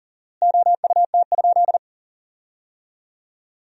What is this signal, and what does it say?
Morse code, heard as a keyed tone, about 37 wpm, 700 Hz. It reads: OUT?
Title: 4x Word-Spacing - Sentences from Top 100 Words - 40wpm